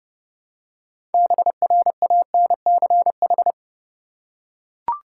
Morse code, heard as a keyed tone, about 30 wpm, 700 Hz. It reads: BRANCH
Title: Six Letter Words - 30wpm